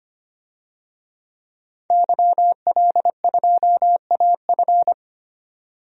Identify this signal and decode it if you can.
Morse code, heard as a keyed tone, about 25 wpm, 700 Hz. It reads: YL2AF
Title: DXCC-100 Call Signs - 5 Characters - 25wpm